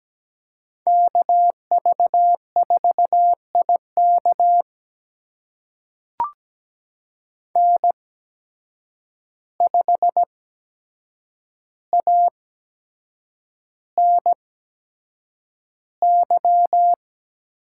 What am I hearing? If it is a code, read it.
Morse code, heard as a keyed tone, about 17 wpm, spaced as for 7 wpm, 700 Hz. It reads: KV4IK N5ANY
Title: USA Call Signs - 5 Characters - 8x Character Spacing - 17wpm